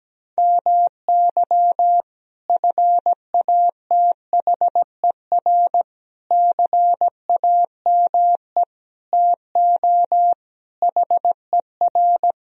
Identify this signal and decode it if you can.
Morse code, heard as a keyed tone, about 17 wpm, 700 Hz. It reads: MY FATHER CAME TO HER
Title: Sentences from Top 1000 Words - Review Part 1 - 17wpm